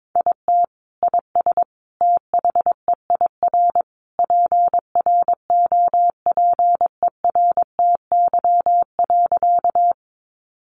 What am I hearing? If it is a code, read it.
Morse code, heard as a keyed tone, about 22 wpm, 700 Hz. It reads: IT IS THEIR PROPERTY.
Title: Sentences from Top 1000 Words 22wpm